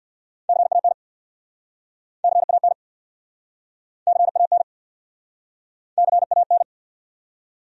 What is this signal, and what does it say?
Morse code, heard as a keyed tone, about 69 wpm, 700 Hz. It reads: CAN CAN CAN CAN